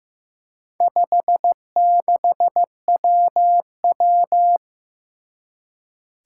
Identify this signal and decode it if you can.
Morse code, heard as a keyed tone, about 15 wpm, 700 Hz. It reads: 56WW
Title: Four Challenging Letter-Number Combinations 15wpm